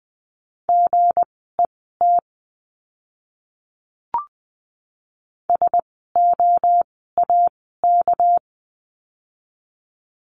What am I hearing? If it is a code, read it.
Morse code, heard as a keyed tone, about 20 wpm, spaced as for 14 wpm, 700 Hz. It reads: GET SOAK
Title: 199 Lesson 32f - Letter k - Words 20wpm